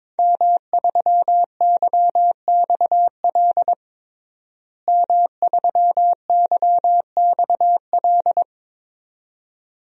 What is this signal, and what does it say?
Morse code, heard as a keyed tone, about 22 wpm, 700 Hz. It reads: M3YXL M3YXL
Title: DXCC-100 Call Signs - 5 Characters - Repeated 2x - 22wpm